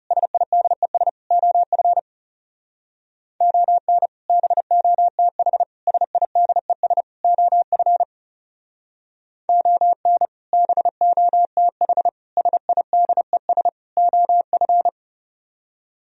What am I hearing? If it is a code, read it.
Morse code, heard as a keyed tone, about 37 wpm, 700 Hz. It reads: SIDES OF ON BOTH SIDES OF ON BOTH SIDES OF
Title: Speed-Racing - Sets of 4 Words 30wpm